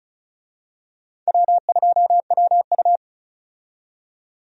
Morse code, heard as a keyed tone, about 34 wpm, 700 Hz. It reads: W2WU
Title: USA Call Signs - 4 Characters - 35wpm